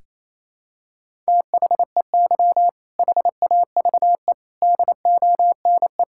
Morse code, heard as a keyed tone, about 28 wpm, 700 Hz. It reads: THEY HAVE DONE
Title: Sentences from Top 1000 Words - Review Part 5 - 28wpm